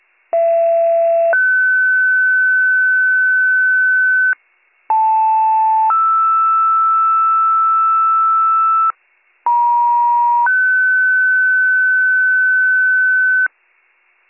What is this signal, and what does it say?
Two-tone page, heard as a one-second tone, then a three-second tone, about 670 then 1500 Hz, about 880 then 1320 Hz, about 950 then 1500 Hz. It is heard from 0.3 to 13.5 s.